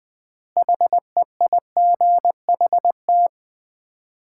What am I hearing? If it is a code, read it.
Morse code, heard as a keyed tone, about 20 wpm, 700 Hz. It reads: HEIGHT
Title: 4x Word-Spacing - Sets of 5 Words - 20wpm